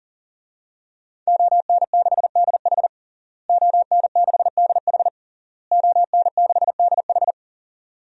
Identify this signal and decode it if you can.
Morse code, heard as a keyed tone, about 40 wpm, 700 Hz. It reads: ON6DH ON6DH ON6DH
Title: DXCC-100 Call Signs - 5 Characters - Repeated 3x - 40wpm